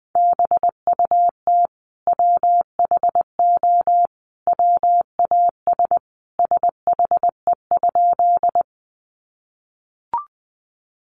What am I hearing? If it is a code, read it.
Morse code, heard as a keyed tone, about 20 wpm, 700 Hz. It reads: BUT WHO WAS SHE?
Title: Sentences from Top 100 Words 20wpm